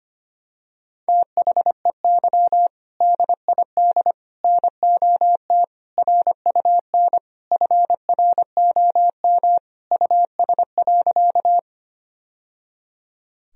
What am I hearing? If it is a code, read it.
Morse code, heard as a keyed tone, about 25 wpm, 700 Hz. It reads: THEY DID NOT RUN FROM US.
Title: Sentences from Top 1000 Words - Review Part 1 - 25wpm